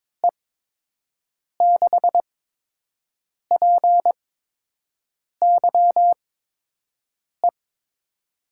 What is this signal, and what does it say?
Morse code, heard as a keyed tone, about 22 wpm, spaced as for 6 wpm, 700 Hz. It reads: E6PYE
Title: USA Call Signs - 6 Characters - 8x Character Spacing - 22wpm